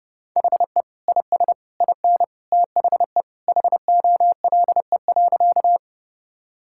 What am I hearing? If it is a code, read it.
Morse code, heard as a keyed tone, about 30 wpm, 700 Hz. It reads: HE IS IN THE HOLE.
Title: Sentences from Top 1600 Words 30wpm